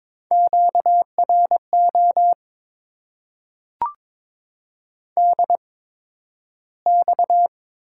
Morse code, heard as a keyed tone, about 22 wpm, spaced as for 12 wpm, 700 Hz. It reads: QRO DX
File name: Top 90 QSO Elements - 8x Character Spacing - 22wpm